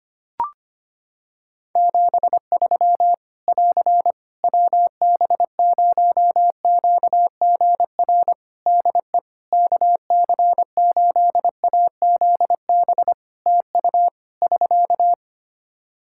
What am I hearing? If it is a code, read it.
Morse code, heard as a keyed tone, about 25 wpm, 700 Hz. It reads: 73 <AR> WB0QGR DE KC8AZB TU <SK>
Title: Practice QSO - Exchange 4 - Ending - 25wpm